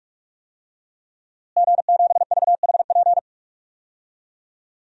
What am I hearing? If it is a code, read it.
Morse code, heard as a keyed tone, about 46 wpm, 700 Hz. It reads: G7VHP
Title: DXCC-100 Call Signs - 5 Characters - Repeated 2x - 45wpm